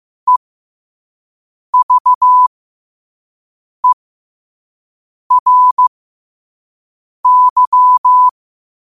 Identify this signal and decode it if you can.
Morse code, heard as a keyed tone, about 15 wpm, spaced as for 5 wpm, 1000 Hz. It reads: EVERY
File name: qotdfiles / 5wpm